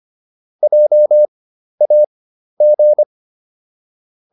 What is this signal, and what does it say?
Morse code, heard as a keyed tone, about 25 wpm, spaced as for 12 wpm, 600 Hz. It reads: JAG